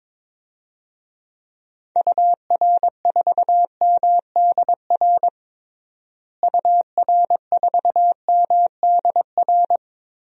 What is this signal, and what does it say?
Morse code, heard as a keyed tone, about 22 wpm, 700 Hz. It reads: UR4MDR UR4MDR
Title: DXCC-100 Call Signs - 6 Characters - Repeated 3x - 22wpm